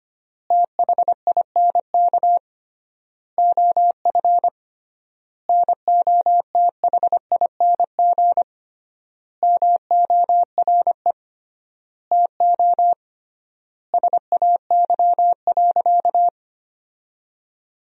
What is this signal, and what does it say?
Morse code, heard as a keyed tone, about 25 wpm, 700 Hz. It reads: THINK OF NOTHING MORE TO SAY.
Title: 3x Word-Spacing - Sentences from Top 500 Words - 25wpm